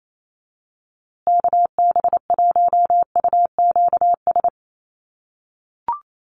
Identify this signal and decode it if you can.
Morse code, heard as a keyed tone, about 28 wpm, 700 Hz. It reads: KB1UQS
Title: USA Call Signs Repeated 3x 28wpm